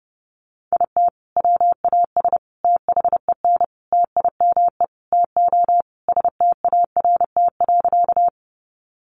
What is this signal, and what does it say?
Morse code, heard as a keyed tone, about 30 wpm, 700 Hz. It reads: IT WAS THEN TIME TO START.
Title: Sentences from Top 300 Words 30wpm